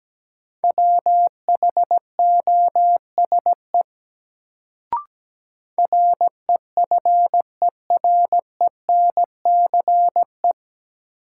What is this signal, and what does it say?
Morse code, heard as a keyed tone, about 17 wpm, 700 Hz. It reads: WHOSE REFERENCE
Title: Sentences from Top 1200 Words - New Word Drill - 17wpm